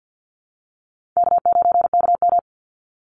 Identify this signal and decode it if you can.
Morse code, heard as a keyed tone, about 50 wpm, 700 Hz. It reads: K9KM
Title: Easy Call Signs Repeated 3x - 50wpm